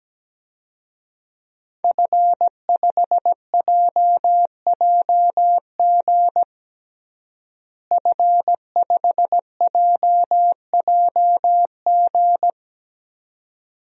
Morse code, heard as a keyed tone, about 17 wpm, 700 Hz. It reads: F5JJG F5JJG
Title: DXCC-100 Call Signs - 5 Characters - Repeated 2x - 17wpm